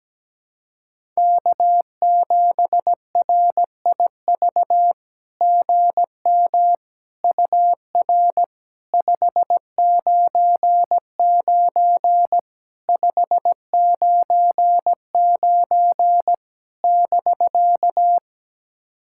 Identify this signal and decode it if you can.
Morse code, heard as a keyed tone, about 17 wpm, 700 Hz. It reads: K7RIV GM UR 599 599 <BK>
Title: SOTA QSO Practice 17wpm